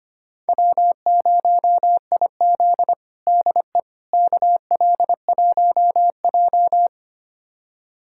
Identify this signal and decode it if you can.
Morse code, heard as a keyed tone, about 25 wpm, 700 Hz. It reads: W0IZ DE KL1J
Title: USA Callsign DE Callsign - 4 Characters - 25wpm